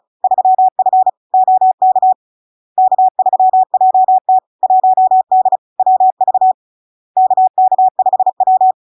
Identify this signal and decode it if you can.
Morse code, heard as a keyed tone, about 36 wpm, 750 Hz. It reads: 3F OK K3JT 1D WV KK5W
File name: LARK-FD-2022-Log-in-CW-35wpm-750-Hz